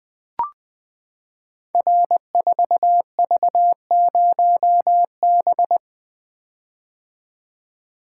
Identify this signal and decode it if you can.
Morse code, heard as a keyed tone, about 20 wpm, 700 Hz. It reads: R4V0B
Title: Code Groups - 5 Letter:Numbers - 20wpm